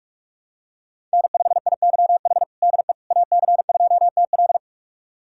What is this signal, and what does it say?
Morse code, heard as a keyed tone, about 46 wpm, 700 Hz. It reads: N5IYH DE AC2TL
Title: USA Callsign DE Callsign - 5 Characters - 45wpm